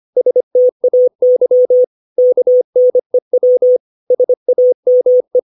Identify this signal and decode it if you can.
Morse code, heard as a keyed tone, about 25 wpm, 500 Hz. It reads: STAY KNEW SAME
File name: 632 words / 632 Words 25WPM